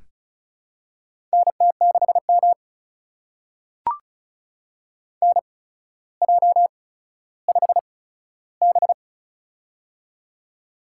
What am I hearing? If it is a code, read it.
Morse code, heard as a keyed tone, about 36 wpm, spaced as for 15 wpm, 700 Hz. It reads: NT6M NJ5B
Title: USA Call Signs - 4 Characters - 8x Character Spacing - 35wpm